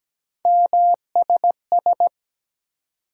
Morse code, heard as a keyed tone, about 17 wpm, 700 Hz. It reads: MSS
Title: USA Call Sign Postfixes - 3 Letters 17wpm